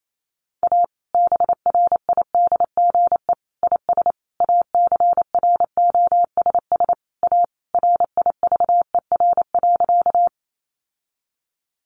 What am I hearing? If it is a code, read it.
Morse code, heard as a keyed tone, about 28 wpm, 700 Hz. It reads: A BRIDGE IS ACROSS A RIVER.
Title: Sentences from Top 1200 Words 28wpm